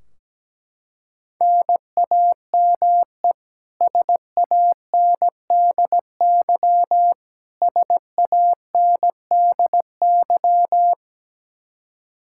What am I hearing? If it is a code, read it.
Morse code, heard as a keyed tone, about 17 wpm, 700 Hz. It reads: NAME SANDY SANDY
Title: Practice QSO - Exchange 1 - Beginning - 17wpm